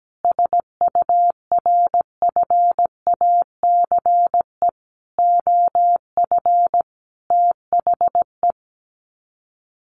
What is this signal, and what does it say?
Morse code, heard as a keyed tone, about 17 wpm, 700 Hz. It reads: SURFACE OF THE
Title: Sets of 5 Words Encore 17wpm